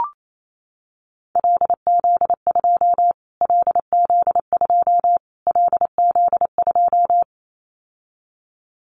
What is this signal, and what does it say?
Morse code, heard as a keyed tone, about 28 wpm, 700 Hz. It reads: LZ2 LZ2 LZ2
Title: Callsign Prefixes for Top 25 DXCC - Repeated 3x - 28wpm